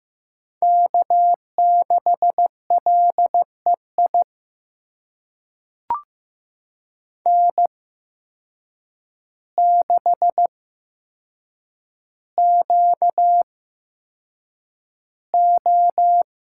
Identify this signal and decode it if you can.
Morse code, heard as a keyed tone, about 15 wpm, spaced as for 7 wpm, 700 Hz. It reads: K6LEI N6QO
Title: USA Call Signs - 5 Characters - 8x Character Spacing - 15wpm